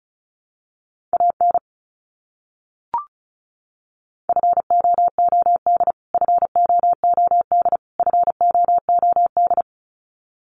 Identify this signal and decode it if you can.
Morse code, heard as a keyed tone, about 34 wpm, 700 Hz. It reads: AN FOOD FOOD FOOD
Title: Top 300 Words Repeated 35wpm